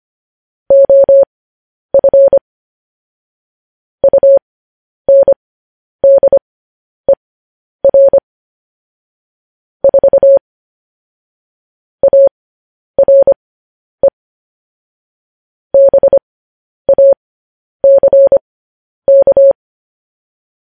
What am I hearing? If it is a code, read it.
Morse code, heard as a keyed tone, about 25 wpm, spaced as for 10 wpm, 550 Hz. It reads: OF UNDER 4 ARE BACK
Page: news headlines in morse code at 10 wpm